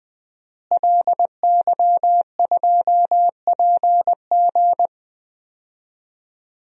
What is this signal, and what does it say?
Morse code, heard as a keyed tone, about 20 wpm, 700 Hz. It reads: LY2PG
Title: DXCC-100 Call Signs - 5 Characters - Repeated 2x - 20wpm